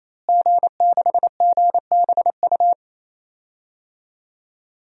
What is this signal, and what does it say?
Morse code, heard as a keyed tone, about 28 wpm, 700 Hz. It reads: G6GBU
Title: DXCC-100 Call Signs - 5 Characters - Repeated 3x - 28wpm